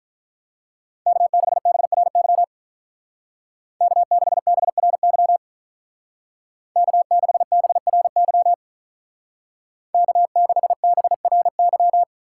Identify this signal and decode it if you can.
Morse code, heard as a keyed tone, about 44 wpm, 700 Hz. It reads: K6BRY K6BRY K6BRY K6BRY